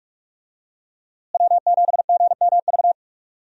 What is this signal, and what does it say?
Morse code, heard as a keyed tone, about 47 wpm, 700 Hz. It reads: W7GMV
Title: USA Call Signs - 5 Characters - Repeated 2x - 45wpm